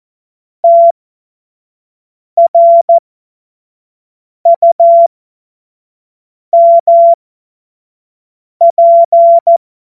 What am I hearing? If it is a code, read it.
Morse code, heard as a keyed tone, about 14 wpm, spaced as for 5 wpm, 680 Hz. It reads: TRUMP
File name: Morsekurs / Witze / Witz 02. November 2021, 25 F70